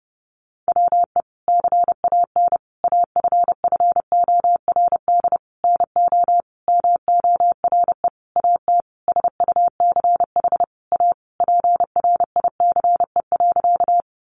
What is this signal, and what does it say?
Morse code, heard as a keyed tone, about 30 wpm, 700 Hz. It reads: WE CAN AFFORD NO MORE AT SUCH A PRICE.
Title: Sentences from Top 1100 Words 30wpm